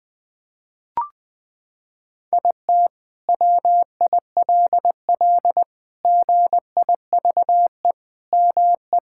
Morse code, heard as a keyed tone, about 20 wpm, 700 Hz. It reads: IT WILL GIVE ME
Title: Sentences from Top 1600 Words 20wpm